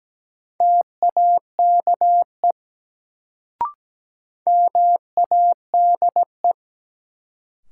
Morse code, heard as a keyed tone, about 17 wpm, 700 Hz. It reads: TAKE MADE
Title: Sentences from Top 200 Words - New Word Drill - 17wpm